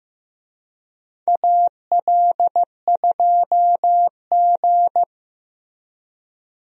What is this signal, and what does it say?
Morse code, heard as a keyed tone, about 15 wpm, 700 Hz. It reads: AL2G
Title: USA Call Signs - 4 Characters - 15wpm